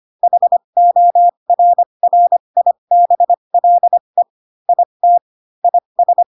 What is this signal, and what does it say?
Morse code, heard as a keyed tone, about 25 wpm, spaced as for 22 wpm, 700 Hz. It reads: HORRIBLE IT IS